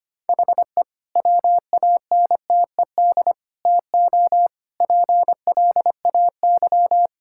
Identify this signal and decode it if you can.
Morse code, heard as a keyed tone, about 25 wpm, 700 Hz. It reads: HE WANTED TO PLAY
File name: practice / 25wpm / Sentences with 3-Word Phrases - 25wpm